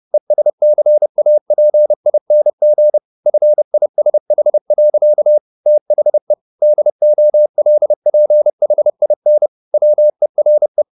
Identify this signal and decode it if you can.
Morse code, heard as a keyed tone, about 30 wpm, 600 Hz. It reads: ESCAPING FISH. THE DOLPHIN WERE
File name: the old man and the sea / the old man and the sea 1 30wpm